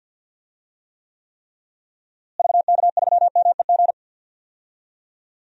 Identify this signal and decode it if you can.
Morse code, heard as a keyed tone, about 46 wpm, 700 Hz. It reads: VK3MEG
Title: DXCC-100 Call Signs - 6 Characters - Repeated 3x - 50wpm